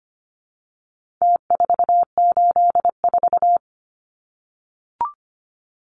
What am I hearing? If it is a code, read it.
Morse code, heard as a keyed tone, about 25 wpm, 700 Hz. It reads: T484
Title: Numbers - Contest Style - 4-Digit 25wpm